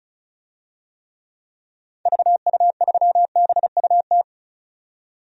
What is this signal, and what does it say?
Morse code, heard as a keyed tone, about 34 wpm, 700 Hz. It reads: VU3BUT
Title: DXCC-100 Call Signs - 6 Characters - 35wpm